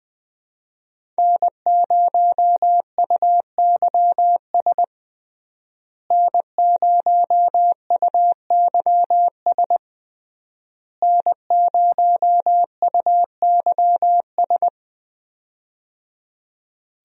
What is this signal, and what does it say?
Morse code, heard as a keyed tone, about 20 wpm, 700 Hz. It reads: N0UYS N0UYS N0UYS
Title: USA Call Signs - 5 Characters - Repeated 3x - 20wpm